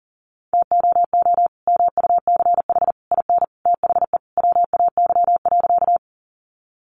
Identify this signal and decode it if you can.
Morse code, heard as a keyed tone, about 40 wpm, 700 Hz. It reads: TOO MUCH IN THE WAY.
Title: Sentences from Top 200 Words 40wpm